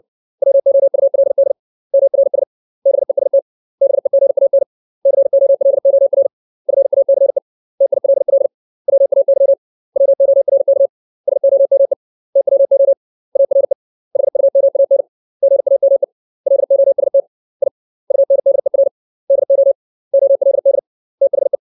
Morse code, heard as a keyed tone, about 60 wpm, 550 Hz. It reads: WOULD MRS BEST BEGAN COLOR FACE TILL WAY WORK SOME TWO ARE HUMAN GAME LOST I UNDER DO OLD THE
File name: 632 words / 632 Words 60WPM